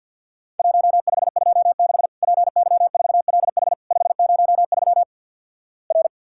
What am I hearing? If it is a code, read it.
Morse code, heard as a keyed tone, about 50 wpm, 700 Hz. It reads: 1526 PY4LH 503 R
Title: Worked All Europe DX Contest - QTC Exchange - 50wpm